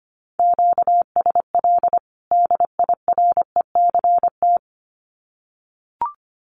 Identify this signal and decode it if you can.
Morse code, heard as a keyed tone, about 25 wpm, 700 Hz. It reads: QSL DIRECT
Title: Common QSO Phrases 25wpm